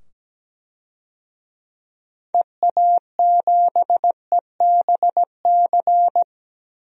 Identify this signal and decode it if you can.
Morse code, heard as a keyed tone, about 17 wpm, 700 Hz. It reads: EA7EBC
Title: DXCC-100 Call Signs - 6 Characters - 17wpm